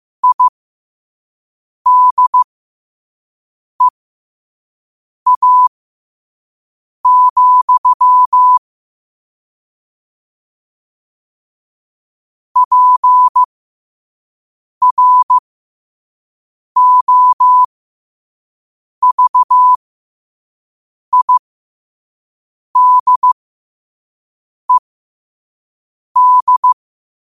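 Morse code, heard as a keyed tone, about 15 wpm, spaced as for 5 wpm, 1000 Hz. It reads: IDEA, PROVIDED
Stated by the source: Morse code at 5 words per minute.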